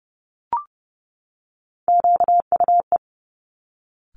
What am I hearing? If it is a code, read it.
Morse code, heard as a keyed tone, about 30 wpm, 690 Hz. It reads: QUE
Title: Top Three Letter Combinations in Words 30wpm